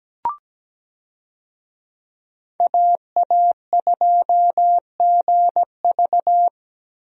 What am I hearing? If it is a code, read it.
Morse code, heard as a keyed tone, about 17 wpm, 700 Hz. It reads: AA2GV
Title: USA Call Signs - 5 Characters - 17wpm